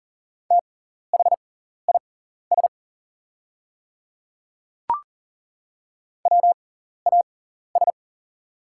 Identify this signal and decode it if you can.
Morse code, heard as a keyed tone, about 38 wpm, spaced as for 14 wpm, 690 Hz. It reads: THIS WAS
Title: 6x Character Spacing - Top 100 Words - 40wpm